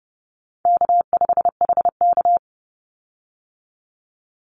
Morse code, heard as a keyed tone, about 32 wpm, 700 Hz. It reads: K5HK
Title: Easy Call Signs - 30wpm